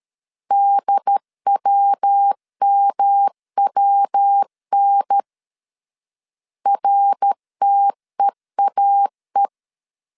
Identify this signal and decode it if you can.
Morse code, heard as a keyed tone, about 13 wpm, 800 Hz. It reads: DWMWN RTEAE
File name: CW / DET / 1 3 1500C S C 12wpm 2750